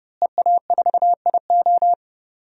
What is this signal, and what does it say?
Morse code, heard as a keyed tone, about 30 wpm, 700 Hz. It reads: EA4IO
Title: DXCC-100 Call Signs - 5 Characters - Repeated 2x - 30wpm